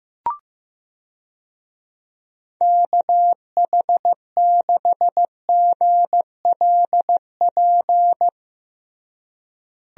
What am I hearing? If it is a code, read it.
Morse code, heard as a keyed tone, about 15 wpm, 700 Hz. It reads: KH6GLP
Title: DXCC-100 Call Signs - 6 Characters - Repeated 2x - 15wpm